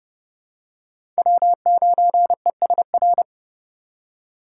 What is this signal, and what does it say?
Morse code, heard as a keyed tone, about 30 wpm, 700 Hz. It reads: W9ESR
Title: USA Call Signs - 3x Character Spacing - 30wpm